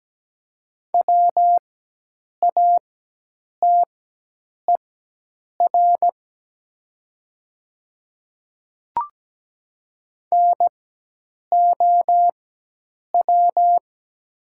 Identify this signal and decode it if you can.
Morse code, heard as a keyed tone, about 17 wpm, spaced as for 8 wpm, 690 Hz. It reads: WATER NOW
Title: 4x Character Spacing - Top 100 Words - 17wpm